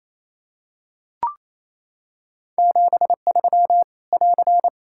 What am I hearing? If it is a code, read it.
Morse code, heard as a keyed tone, about 28 wpm, 700 Hz. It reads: 73 <AR>
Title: Practice QSO - Exchange 4 - Ending - 28wpm